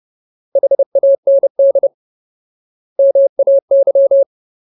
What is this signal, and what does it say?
Morse code, heard as a keyed tone, about 30 wpm, 550 Hz. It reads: HAND MAY
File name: Ultimate Word List / 30WPM Ultimate